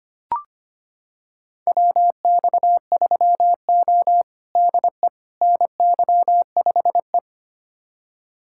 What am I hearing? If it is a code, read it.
Morse code, heard as a keyed tone, about 25 wpm, 700 Hz. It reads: WX3O DE NY5E
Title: USA Callsign DE Callsign - 4 Characters - 25wpm